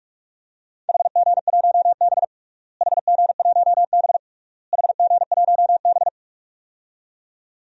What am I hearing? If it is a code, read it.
Morse code, heard as a keyed tone, about 46 wpm, 700 Hz. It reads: HG1B HG1B HG1B
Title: DXCC-100 Call Signs - 4 Characters - Repeated 3x - 45wpm